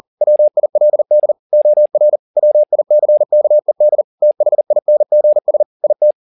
Morse code, heard as a keyed tone, about 40 wpm, 600 Hz. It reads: WILD OR WICKED THINGS IT